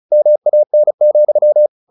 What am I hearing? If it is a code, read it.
Morse code, heard as a keyed tone, about 34 wpm, 600 Hz. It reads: MAN,
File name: the old man and the sea 2 35wpm